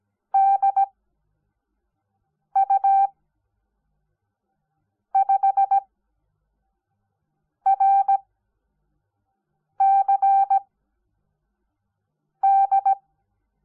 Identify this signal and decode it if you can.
Morse code, heard as a keyed tone, about 17 wpm, spaced as for 4 wpm, 780 Hz. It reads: DU5RCD
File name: Onlinelehrgang / morsen / DJ4UF Morsen05